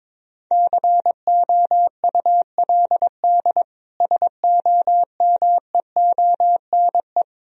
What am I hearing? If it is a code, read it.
Morse code, heard as a keyed tone, about 22 wpm, 700 Hz. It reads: COULD SOMEONE
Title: Sentences from Top 1000 Words - Review Part 4 - 22wpm